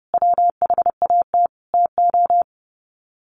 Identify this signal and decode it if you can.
Morse code, heard as a keyed tone, about 30 wpm, 700 Hz. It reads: WHAT TO
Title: Sets of 5 Words 30wpm